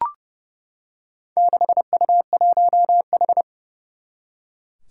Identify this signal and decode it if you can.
Morse code, heard as a keyed tone, about 28 wpm, 700 Hz. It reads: 6U1H
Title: Four Challenging Letter-Number Combinations 30wpm